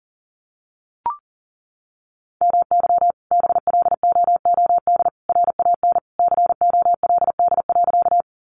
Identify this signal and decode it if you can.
Morse code, heard as a keyed tone, about 40 wpm, 700 Hz. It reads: MY BLOOD RAN COLD.
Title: Sentences from Top 1100 Words 40wpm